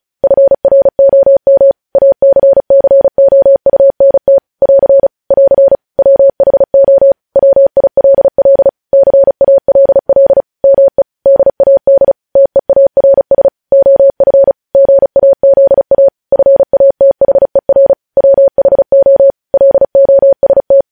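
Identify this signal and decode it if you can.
Morse code, heard as a keyed tone, about 36 wpm, 550 Hz. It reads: FROM ACCOUNT <AR> <AR> WHO WILL CALL ME DAD TEARS OF GAZA FATHER WHO LOST
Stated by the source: News headlines sent in Morse at 35 WPM.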